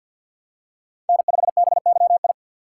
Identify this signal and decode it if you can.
Morse code, heard as a keyed tone, about 47 wpm, 700 Hz. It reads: N5BYI
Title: USA Call Signs - 3x Character Spacing - 50wpm